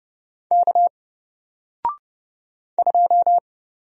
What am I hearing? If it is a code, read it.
Morse code, heard as a keyed tone, about 30 wpm, 700 Hz. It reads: K 2
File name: Single-Letter-Number-30wpm